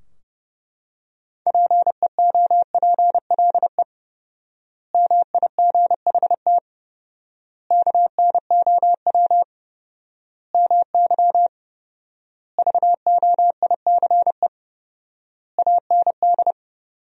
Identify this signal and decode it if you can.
Morse code, heard as a keyed tone, about 32 wpm, 700 Hz. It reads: PEOPLE MIGHT KNOW MY VOICE AND
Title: 4x Word-Spacing - Sentences from Top 500 Words - 30wpm